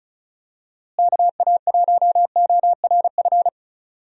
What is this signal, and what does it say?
Morse code, heard as a keyed tone, about 34 wpm, 700 Hz. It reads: KA1ORF
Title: USA Call Signs - 6 Characters - Repeated 3x - 35wpm